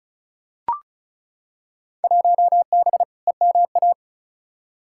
Morse code, heard as a keyed tone, about 36 wpm, 700 Hz. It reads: 1B EMA
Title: Field Day Exchanges - 35wpm